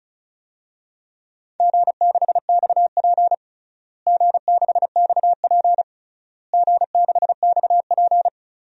Morse code, heard as a keyed tone, about 36 wpm, 700 Hz. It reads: G6XP G6XP G6XP
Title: DXCC-100 Call Signs - 4 Characters - Repeated 3x - 35wpm